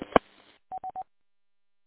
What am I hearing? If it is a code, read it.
There is no code.